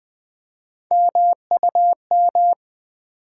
Morse code, heard as a keyed tone, about 20 wpm, 700 Hz. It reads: MUM
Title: Build A Word - 1 Letter At A Time - 3-Letter Words - 20wpm